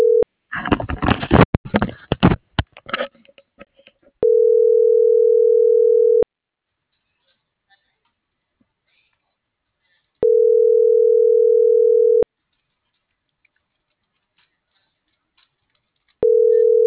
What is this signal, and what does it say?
Ringback tone (North America), 440 + 480 Hz.
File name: CENTRAL MULTIMODAL / out karina.deaquino